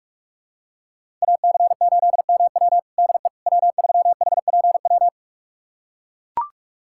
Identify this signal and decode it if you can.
Morse code, heard as a keyed tone, about 41 wpm, 700 Hz. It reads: AC8MW DE W3HPW
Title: USA Callsign DE Callsign - 5 Characters - 45wpm